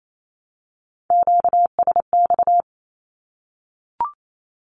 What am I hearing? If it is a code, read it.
Morse code, heard as a keyed tone, about 28 wpm, 700 Hz. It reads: QSX
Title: Q-Codes Repeated 3x 28wpm